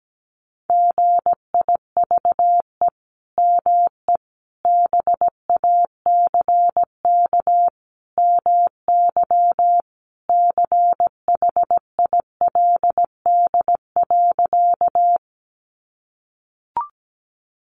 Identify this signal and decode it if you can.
Morse code, heard as a keyed tone, about 17 wpm, 700 Hz. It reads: GIVE ME BACK MY CHILD.
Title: Sentences from Top 800 Words 17wpm